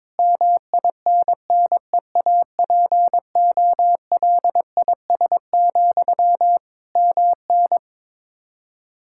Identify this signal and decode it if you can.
Morse code, heard as a keyed tone, about 22 wpm, 700 Hz. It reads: MINNEAPOLIS, MN
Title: Practice QSO - Exchange 1 - Beginning - 22wpm